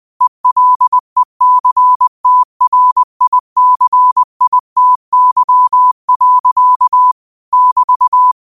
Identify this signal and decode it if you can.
Morse code, heard as a keyed tone, about 20 wpm, 1000 Hz. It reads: ELECTRICITY. <BT>
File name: qotdfiles / 20wpm